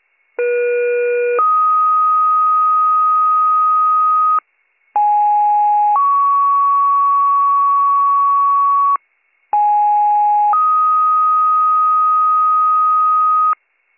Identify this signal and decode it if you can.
Two-tone page, heard as a one-second tone, then a three-second tone, about 480 then 1250 Hz, about 830 then 1120 Hz, about 830 then 1250 Hz. It is heard from 0.4 to 13.5 s.